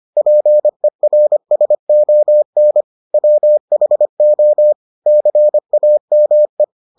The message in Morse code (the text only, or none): PERSON WHO CAME